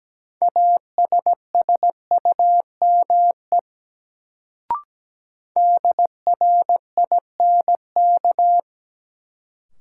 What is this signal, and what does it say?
Morse code, heard as a keyed tone, about 17 wpm, 700 Hz. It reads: ASSUME DRINK
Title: Sentences from Top 1500 Words - New Word Drill - 17wpm